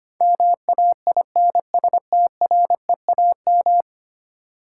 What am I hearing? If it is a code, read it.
Morse code, heard as a keyed tone, about 25 wpm, 700 Hz. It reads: MAINSTREAM
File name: 10 Letter Words - First 3 Letters Given - 25wpm